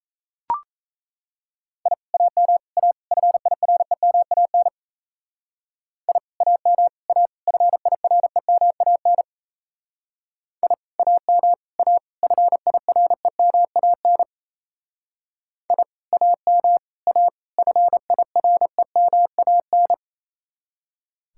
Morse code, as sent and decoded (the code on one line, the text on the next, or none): .. / .- -- / .- / ..-. .. .-. -- .- -. / .. / .- -- / .- / ..-. .. .-. -- .- -. / .. / .- -- / .- / ..-. .. .-. -- .- -. / .. / .- -- / .- / ..-. .. .-. -- .- -.
I AM A FIREMAN I AM A FIREMAN I AM A FIREMAN I AM A FIREMAN